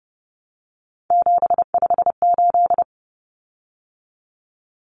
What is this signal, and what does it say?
Morse code, heard as a keyed tone, about 30 wpm, 700 Hz. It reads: 758
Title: Common Cut Numbers - 3-Digit 30wpm